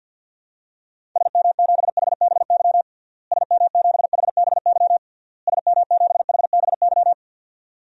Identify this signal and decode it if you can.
Morse code, heard as a keyed tone, about 52 wpm, 700 Hz. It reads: SM7HBY SM7HBY SM7HBY
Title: DXCC-100 Call Signs - 6 Characters - Repeated 3x - 50wpm